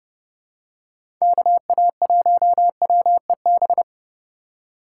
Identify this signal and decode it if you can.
Morse code, heard as a keyed tone, about 30 wpm, 700 Hz. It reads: KA1WEB
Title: USA Call Signs - 3x Character Spacing - 30wpm